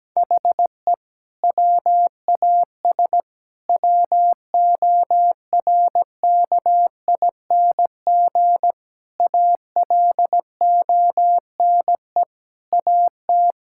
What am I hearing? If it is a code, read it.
Morse code, heard as a keyed tone, about 17 wpm, 700 Hz. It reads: HE WAS WORKING ALONE AT
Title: Sentences from Top 1000 Words - Review Part 5 - 17wpm